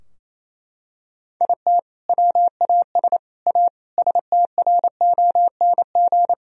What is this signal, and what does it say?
Morse code, heard as a keyed tone, about 28 wpm, 700 Hz. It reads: IT WAS A STRONG
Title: Sentences from Top 1000 Words - Review Part 2 - 28wpm